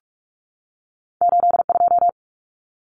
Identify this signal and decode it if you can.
Morse code, heard as a keyed tone, about 46 wpm, 700 Hz. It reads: 82
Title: Common Cut Numbers - 2-Digit 45wpm